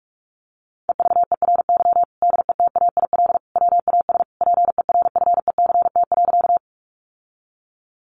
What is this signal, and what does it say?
Morse code, heard as a keyed tone, about 44 wpm, 700 Hz. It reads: EVERY DETAIL WAS PERFECT.
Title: Sentences from Top 1500 Words 45wpm